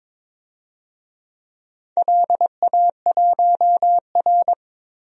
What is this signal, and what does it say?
Morse code, heard as a keyed tone, about 22 wpm, 700 Hz. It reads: LA1R
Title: DXCC-100 Call Signs - 4 Characters - 22wpm